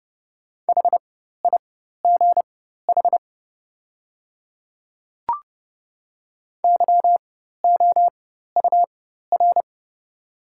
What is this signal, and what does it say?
Morse code, heard as a keyed tone, about 30 wpm, spaced as for 14 wpm, 700 Hz. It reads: HIGH YOUR
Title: 4x Character Spacing - Top 300 Words - 30wpm